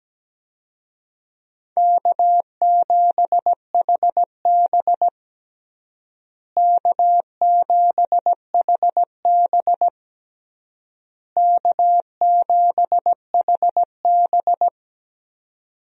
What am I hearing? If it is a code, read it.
Morse code, heard as a keyed tone, about 17 wpm, 700 Hz. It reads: K7HB K7HB K7HB
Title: USA Call Signs - 4 Characters - Repeated 3x - 17wpm